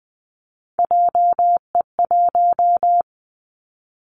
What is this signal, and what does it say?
Morse code, heard as a keyed tone, about 20 wpm, 700 Hz. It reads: JE1
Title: Callsign Prefixes for Top 25 DXCC - 20wpm